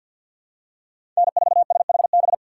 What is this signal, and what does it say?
Morse code, heard as a keyed tone, about 52 wpm, 700 Hz. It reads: N4SHB
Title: USA Call Signs - 5 Characters - 3x Character Spacing - 50wpm